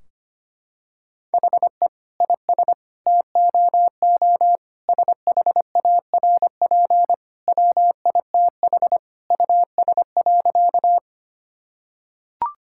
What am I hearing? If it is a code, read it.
Morse code, heard as a keyed tone, about 25 wpm, 700 Hz. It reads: HE IS TOO SHARP WITH US.
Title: Sentences from Top 1600 Words 25wpm